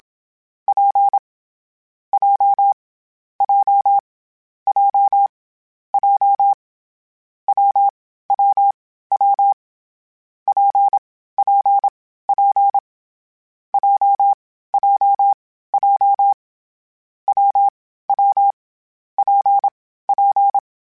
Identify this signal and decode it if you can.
Morse code, heard as a keyed tone, about 27 wpm, 800 Hz. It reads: P J J J J W W W P P P J J J W W P P
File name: Variable Speed / Variable Speed-mp3s / WPJ WPJ1 sp @25